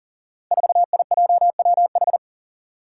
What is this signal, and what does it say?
Morse code, heard as a keyed tone, about 42 wpm, 700 Hz. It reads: 4IJWH
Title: Five Challenging Letter-Number Combinations 40wpm